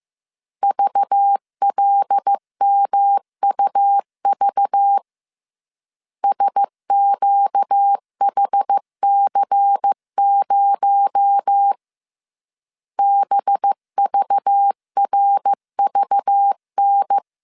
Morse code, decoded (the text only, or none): VLMUV SQHC0 BVRVN